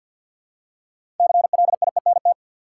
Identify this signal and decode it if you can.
Morse code, heard as a keyed tone, about 51 wpm, 700 Hz. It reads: CLIENT
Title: Six Letter Words - 50wpm